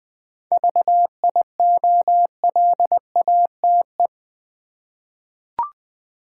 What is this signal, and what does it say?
Morse code, heard as a keyed tone, about 20 wpm, 700 Hz. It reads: VIOLATE
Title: 7 Letter Words - First 3 Letters Given - 20wpm